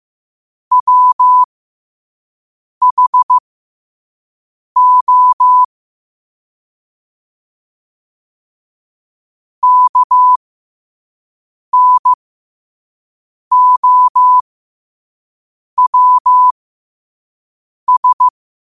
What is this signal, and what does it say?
Morse code, heard as a keyed tone, about 15 wpm, spaced as for 5 wpm, 1000 Hz. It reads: WHO KNOWS
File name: qotdfiles / 5wpm